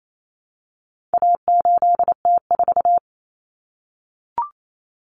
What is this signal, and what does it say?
Morse code, heard as a keyed tone, about 28 wpm, 700 Hz. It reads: A8T4
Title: Common Cut Numbers - 4-Digit 28wpm